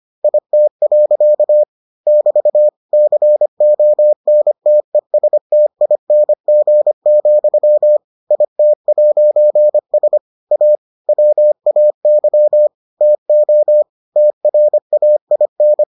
Morse code, heard as a keyed tone, about 25 wpm, 600 Hz. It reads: IT. <BT> CONTESTING, IT'S A WAY TO TRAIN